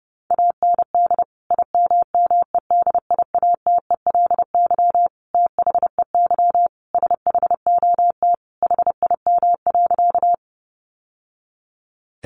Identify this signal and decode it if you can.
Morse code, heard as a keyed tone, about 30 wpm, 700 Hz. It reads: AND IMMEDIATELY THEY SHOT HIM.
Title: Sentences from Top 1100 Words 30wpm